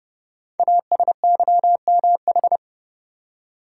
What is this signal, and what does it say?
Morse code, heard as a keyed tone, about 30 wpm, 690 Hz. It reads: ASYMH
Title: Code Groups - 5 Letters - 30wpm